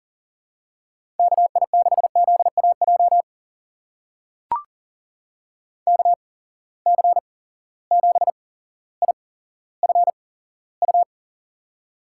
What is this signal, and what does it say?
Morse code, heard as a keyed tone, about 40 wpm, spaced as for 17 wpm, 700 Hz. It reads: KI6ZAJ KC7IFU
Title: USA Call Signs - 6 Characters - 8x Character Spacing - 40wpm